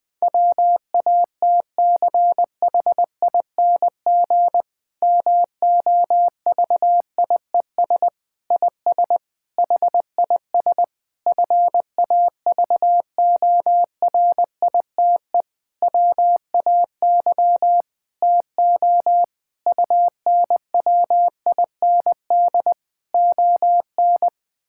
Morse code, heard as a keyed tone, about 20 wpm, 700 Hz. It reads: WATCHING MOVIES IS HIS FAVORITE WAY TO UNWIND ON
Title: Difficult Sentences with Hints - 20wpm